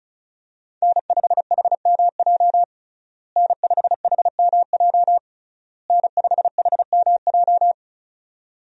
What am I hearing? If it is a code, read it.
Morse code, heard as a keyed tone, about 34 wpm, 700 Hz. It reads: N5HMJ N5HMJ N5HMJ
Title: USA Call Signs - 5 Characters - Repeated 3x - 35wpm